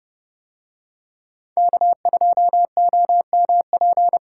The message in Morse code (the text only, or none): K2OMP